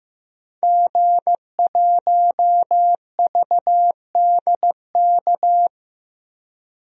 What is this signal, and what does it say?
Morse code, heard as a keyed tone, about 15 wpm, 700 Hz. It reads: G1VDK
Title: DXCC-100 Call Signs - 5 Characters - Repeated 3x - 15wpm